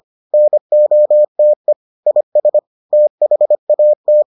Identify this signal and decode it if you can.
Morse code, heard as a keyed tone, about 25 wpm, 600 Hz. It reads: NOTE IS THAT